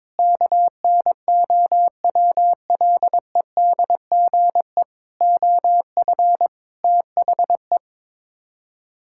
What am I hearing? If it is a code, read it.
Morse code, heard as a keyed tone, about 22 wpm, 700 Hz. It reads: KNOWLEDGE OF THE
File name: practice / 22wpm / Sets of 8 Words - 22wpm